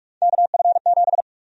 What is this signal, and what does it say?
Morse code, heard as a keyed tone, about 46 wpm, 700 Hz. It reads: KF7
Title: USA Call Sign Prefixes - Repeated 2x - 45wpm